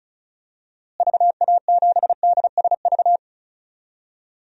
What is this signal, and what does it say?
Morse code, heard as a keyed tone, about 36 wpm, 700 Hz. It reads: VA7DSV
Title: DXCC-100 Call Signs - 6 Characters - Repeated 3x - 35wpm